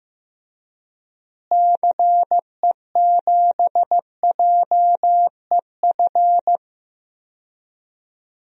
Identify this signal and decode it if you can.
Morse code, heard as a keyed tone, about 15 wpm, 700 Hz. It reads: CE7JEF
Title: DXCC-100 Call Signs - 6 Characters - Repeated 2x - 15wpm